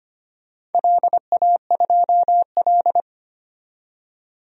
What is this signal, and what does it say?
Morse code, heard as a keyed tone, about 25 wpm, 700 Hz. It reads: LA2L
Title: DXCC-100 Call Signs - 4 Characters - 25wpm